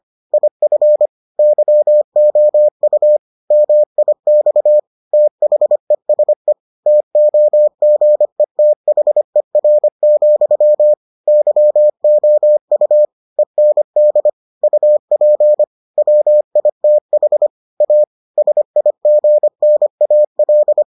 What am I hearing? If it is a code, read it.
Morse code, heard as a keyed tone, about 25 wpm, 600 Hz. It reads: IF YOU MIX THESE TOGETHER, YOU END UP WITH A SIGNAL